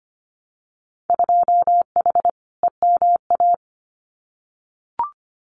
Morse code, heard as a keyed tone, about 25 wpm, 690 Hz. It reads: 2H EMA
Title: Winter Field Day Exchanges 25wpm